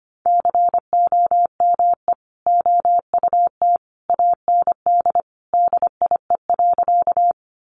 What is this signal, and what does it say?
Morse code, heard as a keyed tone, about 25 wpm, 700 Hz. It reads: COME OUT AND DIE.
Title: Sentences from Top 1300 Words 25wpm